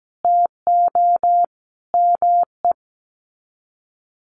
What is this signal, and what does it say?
Morse code, heard as a keyed tone, about 17 wpm, 700 Hz. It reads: TO ME
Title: Sets of 5 Words Encore2 17wpm